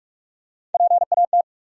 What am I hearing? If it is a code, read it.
Morse code, heard as a keyed tone, about 46 wpm, 700 Hz. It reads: PAT
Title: USA Common Names - Max 5 Letters - 45wpm